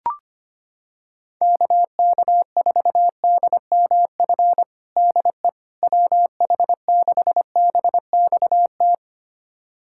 Morse code, heard as a keyed tone, about 25 wpm, 700 Hz. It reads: KK4DMF DE WH6BXT
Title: USA Callsign DE Callsign 25wpm